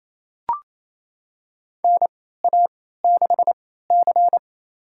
Morse code, heard as a keyed tone, about 27 wpm, spaced as for 16 wpm, 700 Hz. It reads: NA6C
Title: USA Call Signs - 4 Characters - 3x Character Spacing - 28wpm